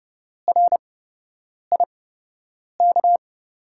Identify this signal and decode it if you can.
Morse code, heard as a keyed tone, about 30 wpm, spaced as for 8 wpm, 700 Hz. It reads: RIK
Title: USA Call Signs - 5 Characters - 8x Character Spacing - 30wpm